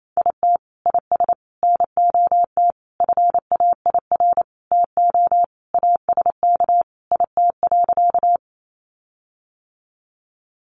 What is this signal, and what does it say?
Morse code, heard as a keyed tone, about 28 wpm, 700 Hz. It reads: IT IS NOT FAIR TO ASK IT.
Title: Sentences from Top 1300 Words 28wpm